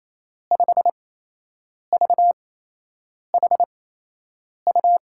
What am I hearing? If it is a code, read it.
Morse code, heard as a keyed tone, about 28 wpm, spaced as for 8 wpm, 700 Hz. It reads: 5VHU
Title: USA Call Signs - 8x Character Spacing - 28wpm